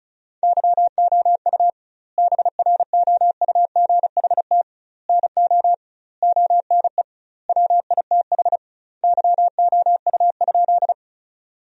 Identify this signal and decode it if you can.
Morse code, heard as a keyed tone, about 36 wpm, 700 Hz. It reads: YOU BROUGHT NO ONE WITH YOU?
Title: 2x Word-Spacing - Sentences from Top 500 Words - 35wpm